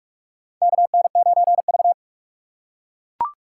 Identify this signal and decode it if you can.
Morse code, heard as a keyed tone, about 41 wpm, 700 Hz. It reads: KN9V
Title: USA Call Signs - 3x Character Spacing - 45wpm